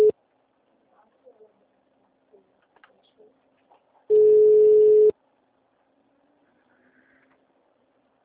Ringback tone (Europe, 425 Hz).
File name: out- -2024- - - .955